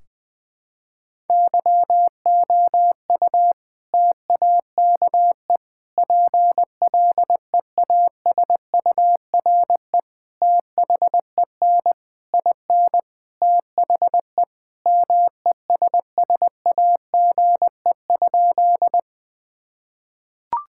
Morse code, heard as a keyed tone, about 21 wpm, 700 Hz. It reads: YOU TAKE PLEASURE THEN IN THE MESSAGE?
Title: Sentences from Top 1600 Words 20wpm